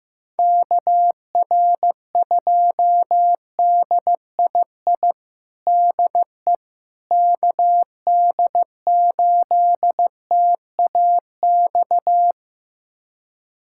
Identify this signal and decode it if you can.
Morse code, heard as a keyed tone, about 16 wpm, 700 Hz. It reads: KR2DII DE KD8TAX
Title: USA Callsign DE Callsign 15wpm